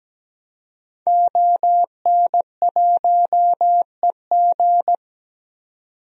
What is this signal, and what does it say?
Morse code, heard as a keyed tone, about 17 wpm, 700 Hz. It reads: ON1EG
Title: DXCC-100 Call Signs - 5 Characters - 17wpm